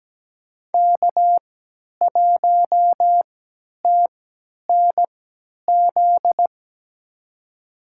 Morse code, heard as a keyed tone, about 17 wpm, spaced as for 10 wpm, 700 Hz. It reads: K1TNZ
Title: USA Call Signs - 5 Characters - 3x Character Spacing - 17wpm